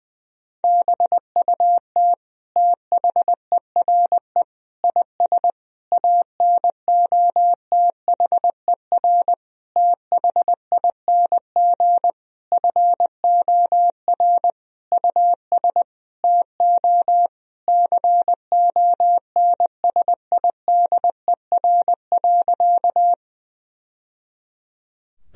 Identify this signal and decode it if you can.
Morse code, heard as a keyed tone, about 20 wpm, 700 Hz. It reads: BUT THERE IS ANOTHER THING FOR US TO CONSIDER.
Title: Sentences from Top 1000 Words - Review Part 4 - 20wpm